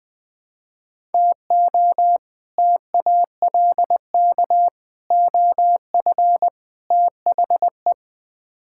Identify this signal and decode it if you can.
Morse code, heard as a keyed tone, about 20 wpm, 700 Hz. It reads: TO TALK OF THE
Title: Build A Phrase - 5-Word Phrases - 20wpm